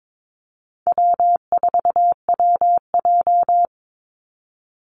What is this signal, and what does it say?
Morse code, heard as a keyed tone, about 22 wpm, 700 Hz. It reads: W4WJ
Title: Call Signs Repeated 2x - 22wpm